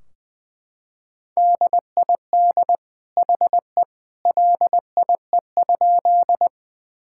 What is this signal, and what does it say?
Morse code, heard as a keyed tone, about 20 wpm, 700 Hz. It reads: DID HE LIE?
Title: Sentences from Top 1600 Words 20wpm